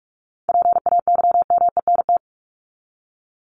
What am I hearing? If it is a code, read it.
Morse code, heard as a keyed tone, about 44 wpm, 700 Hz. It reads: PAYMENT
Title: Top 1001-1250 Words 45wpm